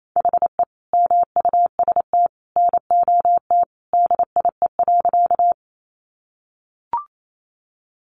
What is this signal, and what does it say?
Morse code, heard as a keyed tone, about 28 wpm, 700 Hz. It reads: HE MUST NOT DIE.
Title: Sentences from Top 1300 Words 28wpm